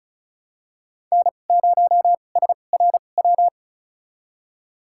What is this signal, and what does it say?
Morse code, heard as a keyed tone, about 34 wpm, spaced as for 25 wpm, 700 Hz. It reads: N0SRW
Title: USA Call Signs - 5 Characters - 2x Character Spacing - 35wpm